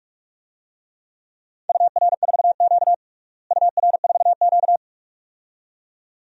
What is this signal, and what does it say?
Morse code, heard as a keyed tone, about 46 wpm, 700 Hz. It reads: UR4Q UR4Q
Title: DXCC-100 Call Signs - 4 Characters - Repeated 2x - 45wpm